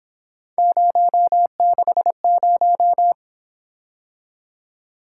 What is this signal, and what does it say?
Morse code, heard as a keyed tone, about 26 wpm, 700 Hz. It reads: 060
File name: Speed-Racing - Three Letter-Number Combinations - 22wpm